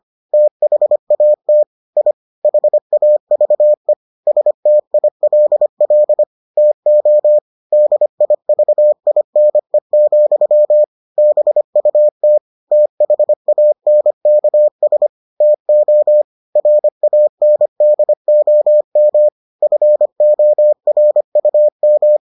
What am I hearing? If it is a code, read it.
Morse code, heard as a keyed tone, about 25 wpm, 600 Hz. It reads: THAT I HAVE STILL TO DIVINE, BUT THANKS TO RANDOM FORUM